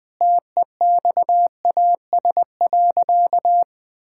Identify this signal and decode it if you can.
Morse code, heard as a keyed tone, about 20 wpm, 700 Hz. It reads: TEXAS.
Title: Sentences with 4-Word Phrases - 20wpm